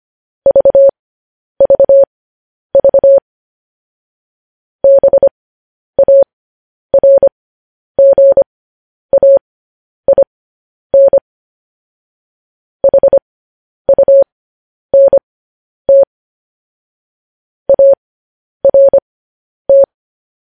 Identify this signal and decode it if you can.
Morse code, heard as a keyed tone, about 25 wpm, spaced as for 10 wpm, 550 Hz. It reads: VVV BARGAIN HUNT ART
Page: News Headlines in Morse Code at 10 WPM